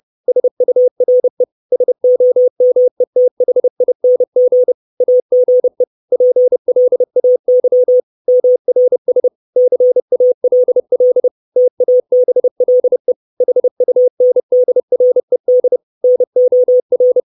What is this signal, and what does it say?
Morse code, heard as a keyed tone, about 30 wpm, 500 Hz. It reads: SURE SOMETHING AGE PLAY MRS CALL TABLE HUNDRED NOR
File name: 632 Words 30WPM